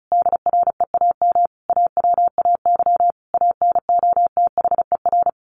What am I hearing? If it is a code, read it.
Morse code, heard as a keyed tone, about 34 wpm, 700 Hz. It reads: DREAM AWAY ANOTHER
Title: Sentences from Top 1500 Words 35wpm